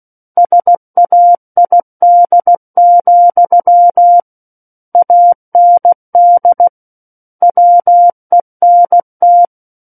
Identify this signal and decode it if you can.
Morse code, heard as a keyed tone, about 16 wpm, 700 Hz. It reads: SAID, AND WENT
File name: Book One / 07-Chapter 7 - 16 WPM